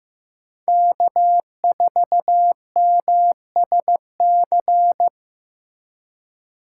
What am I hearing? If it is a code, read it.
Morse code, heard as a keyed tone, about 15 wpm, 700 Hz. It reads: K4MSC
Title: USA Call Signs - 3x Character Spacing - 15wpm